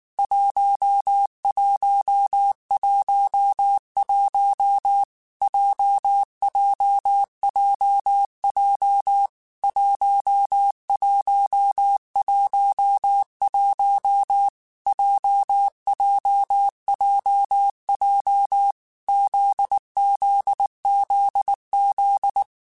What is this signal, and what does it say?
Morse code, heard as a keyed tone, about 19 wpm, 800 Hz. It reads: 1111 JJJJ 1111 JJJJ ZZZZ
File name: Basic Sets / Basic Letter Groups Collective7 / JZ17@18